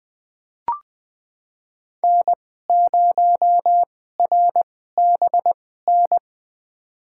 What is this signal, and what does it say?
Morse code, heard as a keyed tone, about 20 wpm, spaced as for 14 wpm, 700 Hz. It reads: N0RBN
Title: USA Call Signs - 5 Characters - 2x Character Spacing - 20wpm